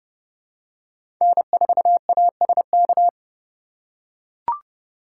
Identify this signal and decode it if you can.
Morse code, heard as a keyed tone, about 30 wpm, 700 Hz. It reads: N4ASK
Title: USA Call Signs - 2x Character Spacing - 30wpm